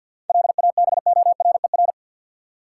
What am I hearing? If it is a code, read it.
Morse code, heard as a keyed tone, about 52 wpm, 700 Hz. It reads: LABORER
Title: Practice QSO - Exchange 3 - More Info - 50wpm